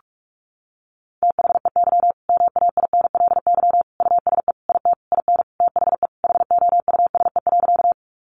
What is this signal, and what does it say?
Morse code, heard as a keyed tone, about 44 wpm, 700 Hz. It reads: THEY MAINLY USE IT IN THE HOUSE.
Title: Sentences from Top 1400 Words 45wpm